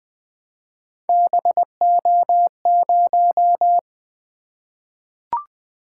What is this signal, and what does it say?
Morse code, heard as a keyed tone, about 20 wpm, 700 Hz. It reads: BO0
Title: Three Letter-Number Combinations - 20wpm